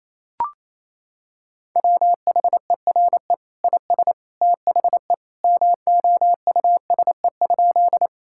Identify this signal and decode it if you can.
Morse code, heard as a keyed tone, about 28 wpm, 700 Hz. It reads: WHERE IS THE MOUSE?
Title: Sentences from Top 1000 Words - Review Part 5 - 28wpm